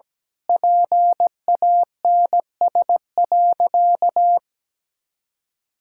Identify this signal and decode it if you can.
Morse code, heard as a keyed tone, about 17 wpm, 700 Hz. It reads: PANS.
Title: Pangram Challenge v2 - 17wpm